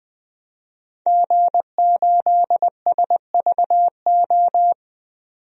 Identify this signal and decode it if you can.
Morse code, heard as a keyed tone, about 20 wpm, 700 Hz. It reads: G8SVO
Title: DXCC-100 Call Signs - 5 Characters - 20wpm